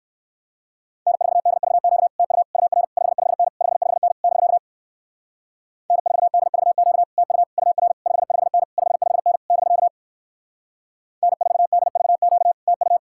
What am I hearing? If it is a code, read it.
Morse code, heard as a keyed tone, about 56 wpm, 700 Hz. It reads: N4DVQ TU UR 55N 55N <BK> N4DVQ TU UR 55N 55N <BK> N4DVQ TU